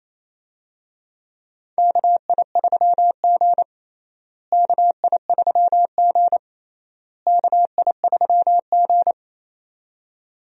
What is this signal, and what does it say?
Morse code, heard as a keyed tone, about 28 wpm, 700 Hz. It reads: KI3G KI3G KI3G
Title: USA Call Signs - 4 Characters - Repeated 3x - 28wpm